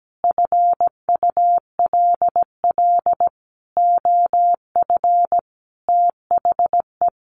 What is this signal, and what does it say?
Morse code, heard as a keyed tone, about 17 wpm, 700 Hz. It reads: FULL OF THE
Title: A Sound Of Thunder 17wpm